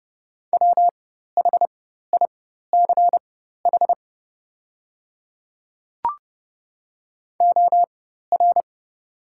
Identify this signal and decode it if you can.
Morse code, heard as a keyed tone, about 30 wpm, spaced as for 14 wpm, 700 Hz. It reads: WHICH OR
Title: 4x Character Spacing - Top 100 Words - 30wpm